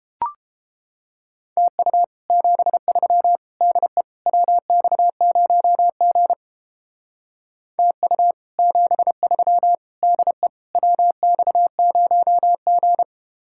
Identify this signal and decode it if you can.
Morse code, heard as a keyed tone, about 29 wpm, 700 Hz. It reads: TU 73 DE WX0G TU 73 DE WX0G